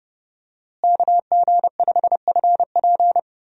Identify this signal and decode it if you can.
Morse code, heard as a keyed tone, about 30 wpm, 700 Hz. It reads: KG5FP
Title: USA Call Signs - 5 Characters - Repeated 3x - 30wpm